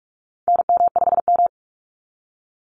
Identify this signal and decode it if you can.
Morse code, heard as a keyed tone, about 44 wpm, 700 Hz. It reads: NM5M
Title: Call Signs Repeated 2x - 45wpm